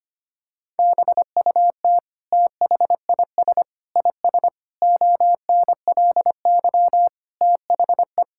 Morse code, heard as a keyed tone, about 25 wpm, 700 Hz. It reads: BUT THIS IS ONLY THE
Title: Sentences from Top 1000 Words - Review Part 5 - 25wpm